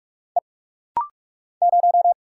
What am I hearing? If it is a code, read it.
Morse code, heard as a keyed tone, about 44 wpm, 700 Hz. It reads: E 0
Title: Single Letter-Number - Rapid-Fire 45wpm